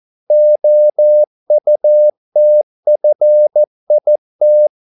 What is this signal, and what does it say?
Morse code, heard as a keyed tone, about 14 wpm, 600 Hz. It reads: OUTFIT